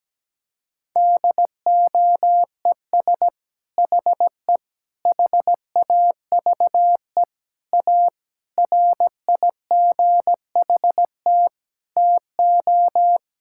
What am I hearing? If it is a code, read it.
Morse code, heard as a keyed tone, about 17 wpm, 700 Hz. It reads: DOES HE HAVE A RIGHT TO
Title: Sentences from Top 1000 Words - Review Part 5 - 17wpm